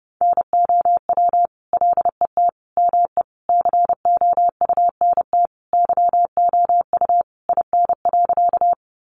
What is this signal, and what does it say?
Morse code, heard as a keyed tone, about 30 wpm, 700 Hz. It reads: NOW LET ME COUNT YOU IN.
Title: Sentences from Top 1500 Words 30wpm